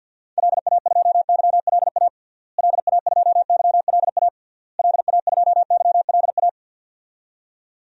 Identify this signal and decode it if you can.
Morse code, heard as a keyed tone, about 52 wpm, 700 Hz. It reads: LA2YLA LA2YLA LA2YLA
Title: DXCC-100 Call Signs - 6 Characters - Repeated 3x - 50wpm